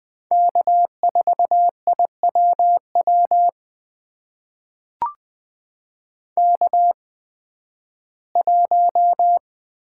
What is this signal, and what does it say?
Morse code, heard as a keyed tone, about 20 wpm, spaced as for 13 wpm, 700 Hz. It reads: K4IWW K1
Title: USA Call Signs - 5 Characters - 8x Character Spacing - 20wpm